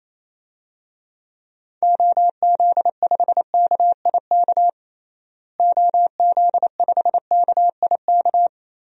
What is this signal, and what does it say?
Morse code, heard as a keyed tone, about 28 wpm, 700 Hz. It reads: OZ5KIK OZ5KIK
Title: DXCC-100 Call Signs - 6 Characters - Repeated 2x - 28wpm